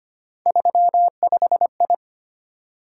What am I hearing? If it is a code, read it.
Morse code, heard as a keyed tone, about 25 wpm, 700 Hz. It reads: 35I
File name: Three Letter-Number Combinations - 25wpm